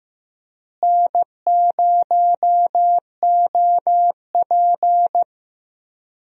Morse code, heard as a keyed tone, about 15 wpm, 700 Hz. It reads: N0OP
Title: USA Call Signs - 4 Characters - Repeated 2x - 15wpm